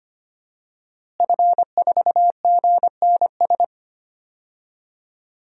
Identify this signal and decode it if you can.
Morse code, heard as a keyed tone, about 25 wpm, 700 Hz. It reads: F4GNS
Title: DXCC-100 Call Signs - 5 Characters - Repeated 3x - 25wpm